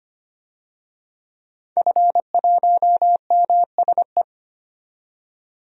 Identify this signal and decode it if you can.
Morse code, heard as a keyed tone, about 25 wpm, 700 Hz. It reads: F1MSE
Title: DXCC-100 Call Signs - 5 Characters - 25wpm